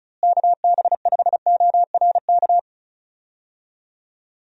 Morse code, heard as a keyed tone, about 36 wpm, 700 Hz. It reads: KB5ORK
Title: USA Call Signs - 6 Characters - Repeated 2x - 35wpm